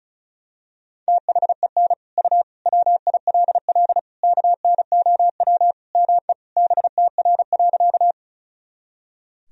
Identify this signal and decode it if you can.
Morse code, heard as a keyed tone, about 36 wpm, 700 Hz. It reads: THEN U WILL KNOW ME BTR.
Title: Sentences from Top 400 Words and Abbreviations 35wpm